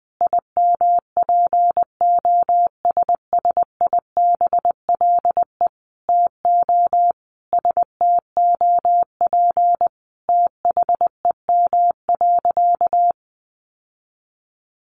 Morse code, heard as a keyed tone, about 20 wpm, 700 Hz. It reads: IMPOSSIBLE TO STOP THEM.
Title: Sentences from Top 1100 Words 20wpm